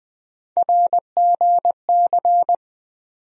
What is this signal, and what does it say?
Morse code, heard as a keyed tone, about 20 wpm, 700 Hz. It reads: RGC
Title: USA Call Sign Postfixes - 3 Letters 20wpm